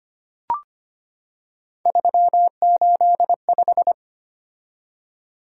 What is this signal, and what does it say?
Morse code, heard as a keyed tone, about 25 wpm, 700 Hz. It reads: 385
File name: Three Letter-Number Combinations - 25wpm